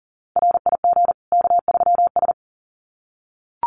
Morse code, heard as a keyed tone, about 40 wpm, 700 Hz. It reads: RIG K3S
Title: Common QSO Phrases 40wpm